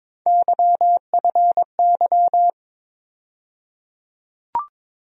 Morse code, heard as a keyed tone, about 22 wpm, 700 Hz. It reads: YFY
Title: USA Call Sign Postfixes - 3 Letters 22wpm